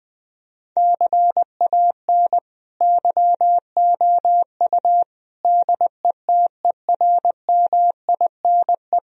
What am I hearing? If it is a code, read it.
Morse code, heard as a keyed tone, about 21 wpm, 700 Hz. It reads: CAN YOU DETERMINE
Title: Sentences from Top 1000 Words - Review Part 4 - 20wpm